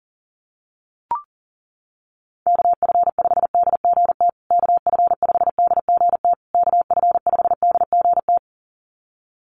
Morse code, heard as a keyed tone, about 40 wpm, 700 Hz. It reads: KF5DGT KF5DGT KF5DGT
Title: USA Call Signs Repeated 3x 40wpm